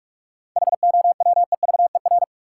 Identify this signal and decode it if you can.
Morse code, heard as a keyed tone, about 44 wpm, 700 Hz. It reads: HOWEVER
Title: Sentences from Top 600 Words - New Word Drill - 45wpm